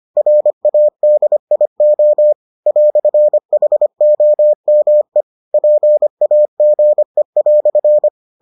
Morse code, heard as a keyed tone, about 25 wpm, 600 Hz. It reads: RADIO "HOME PAGE"